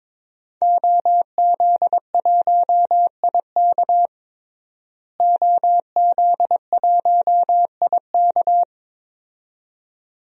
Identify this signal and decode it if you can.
Morse code, heard as a keyed tone, about 22 wpm, 700 Hz. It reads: OZ1IK OZ1IK
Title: DXCC-100 Call Signs - 5 Characters - Repeated 2x - 22wpm